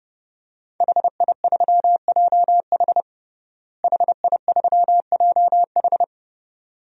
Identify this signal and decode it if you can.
Morse code, heard as a keyed tone, about 32 wpm, 700 Hz. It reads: HI3JH HI3JH
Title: DXCC-100 Call Signs - 5 Characters - Repeated 2x - 30wpm